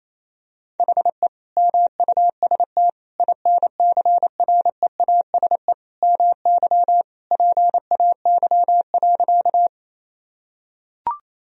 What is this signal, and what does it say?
Morse code, heard as a keyed tone, about 28 wpm, 700 Hz. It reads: HE MUST INCREASE MY PAY.
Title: Sentences from Top 1000 Words - Review Part 4 - 28wpm